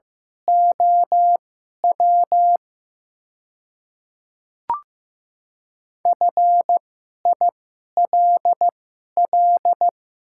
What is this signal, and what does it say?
Morse code, heard as a keyed tone, about 15 wpm, spaced as for 11 wpm, 700 Hz. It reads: OW FILL
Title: 2x Character Spacing - Top 500 Words - 15wpm